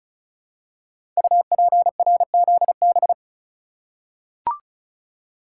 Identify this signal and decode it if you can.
Morse code, heard as a keyed tone, about 36 wpm, 700 Hz. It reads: UPRZB
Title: Code Groups - 5 Letters - 35wpm